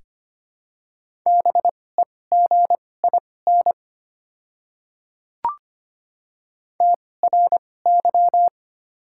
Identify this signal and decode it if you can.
Morse code, heard as a keyed tone, about 25 wpm, spaced as for 18 wpm, 700 Hz. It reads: BEGIN TRY
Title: 2x Character Spacing - Top 500 Words - 25wpm